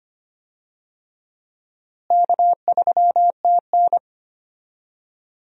Morse code, heard as a keyed tone, about 25 wpm, 700 Hz. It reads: K3TN
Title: USA Call Signs - 4 Characters - Repeated 3x - 25wpm